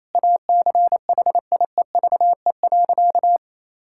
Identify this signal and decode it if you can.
Morse code, heard as a keyed tone, about 28 wpm, 700 Hz. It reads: ACHIEVE.
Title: Sentences from Top 1000 Words - Review Part 6 - 28wpm